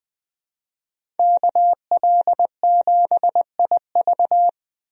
Morse code, heard as a keyed tone, about 20 wpm, 700 Hz. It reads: KL7IV
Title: DXCC-100 Call Signs - 5 Characters - 20wpm